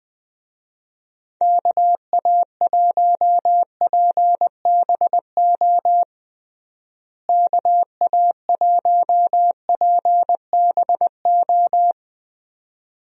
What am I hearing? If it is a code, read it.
Morse code, heard as a keyed tone, about 20 wpm, 700 Hz. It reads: KA1PBO KA1PBO
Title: USA Call Signs - 6 Characters - Repeated 2x - 20wpm